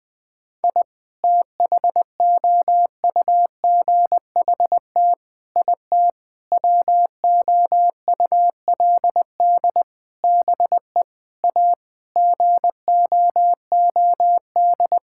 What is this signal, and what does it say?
Morse code, heard as a keyed tone, about 20 wpm, 700 Hz. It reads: I THOUGHT IT WOULD BE A GOOD
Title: Sets of 9 Words - 20wpm